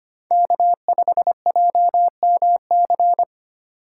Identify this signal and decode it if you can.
Morse code, heard as a keyed tone, about 25 wpm, 700 Hz. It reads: K5JMC
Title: USA Call Signs - 5 Characters - 3x Character Spacing - 25wpm